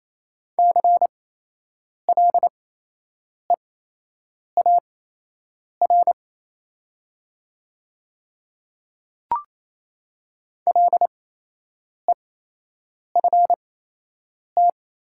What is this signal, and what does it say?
Morse code, heard as a keyed tone, about 28 wpm, spaced as for 8 wpm, 700 Hz. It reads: CLEAR LEFT
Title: 8x Character Spacing - Top 500 Words - 28wpm